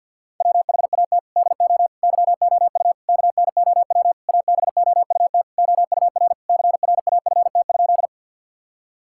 Morse code, heard as a keyed tone, about 50 wpm, 700 Hz. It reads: WHAT DO YOU KNOW ABOUT OUR CRAFT?
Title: Sentences from Top 1000 Words - Review Part 6 - 50wpm